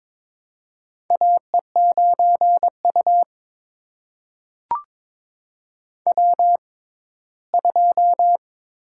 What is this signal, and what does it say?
Morse code, heard as a keyed tone, about 22 wpm, spaced as for 15 wpm, 700 Hz. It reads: AE9U W2
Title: USA Call Signs - 4 Characters - 6x Character Spacing - 22wpm